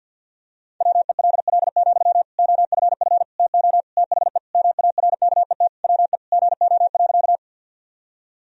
Morse code, heard as a keyed tone, about 47 wpm, 700 Hz. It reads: WELL, OFF TO THE MARKET WE GO.
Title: Sentences from Top 1000 Words - Review Part 3 - 50wpm